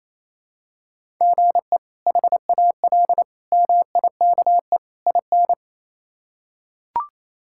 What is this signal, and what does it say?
Morse code, heard as a keyed tone, about 28 wpm, 700 Hz. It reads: GE HAL MIKE IN
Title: K1USN SST QSO Practice - 28wpm